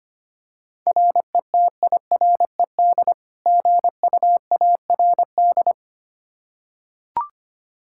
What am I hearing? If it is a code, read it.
Morse code, heard as a keyed tone, about 25 wpm, 700 Hz. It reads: RETIRED GUARD
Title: Practice QSO - Exchange 3 - More Info - 25wpm